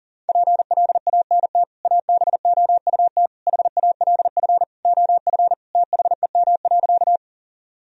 Morse code, heard as a keyed tone, about 37 wpm, 700 Hz. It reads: PLANT ABOUT HALF OF THEM.
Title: Sentences from Top 1000 Words - Review Part 1 - 40wpm